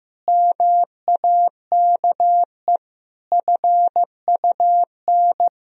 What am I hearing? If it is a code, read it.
Morse code, heard as a keyed tone, about 15 wpm, 700 Hz. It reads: MAKE FUN
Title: Sentences from Top 1000 Words - Review Part 4 - 15wpm